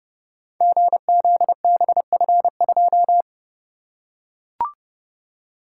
Morse code, heard as a keyed tone, about 28 wpm, 700 Hz. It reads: GZBF2
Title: Code Groups - 5 Letter:Numbers - 30wpm